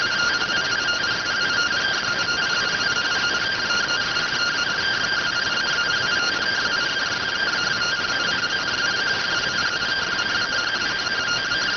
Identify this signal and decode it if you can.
Two-tone teleprinter signal, about 170 Hz shift.